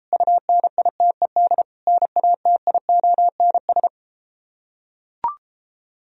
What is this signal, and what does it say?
Morse code, heard as a keyed tone, about 33 wpm, 700 Hz. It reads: UNITED NATIONS